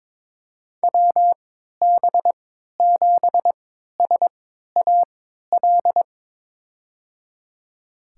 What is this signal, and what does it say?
Morse code, heard as a keyed tone, about 22 wpm, spaced as for 12 wpm, 700 Hz. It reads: WB7SAL
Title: USA Call Signs - 3x Character Spacing - 22wpm